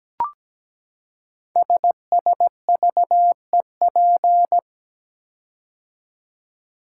Morse code, heard as a keyed tone, about 17 wpm, 700 Hz. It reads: SSVEP
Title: Code Groups - 5 Letters - 17wpm